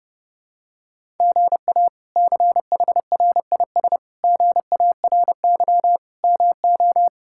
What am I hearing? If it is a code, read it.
Morse code, heard as a keyed tone, about 30 wpm, 700 Hz. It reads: GA CHRIS GARY MO
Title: K1USN SST QSO Practice - 30wpm